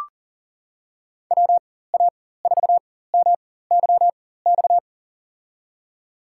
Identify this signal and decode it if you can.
Morse code, heard as a keyed tone, about 40 wpm, spaced as for 19 wpm, 700 Hz. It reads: WA4MYX
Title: USA Call Signs - 6 Characters - 4x Character Spacing - 40wpm